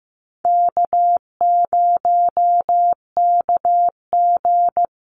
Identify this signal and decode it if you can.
Morse code, heard as a keyed tone, about 15 wpm, 700 Hz. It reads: K0KG
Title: Easy Call Signs Repeated 2x - 15wpm